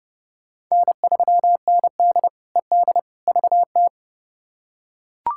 Morse code, heard as a keyed tone, about 32 wpm, 700 Hz. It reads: N3ND ED VT
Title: K1USN SST QSO Practice - 30wpm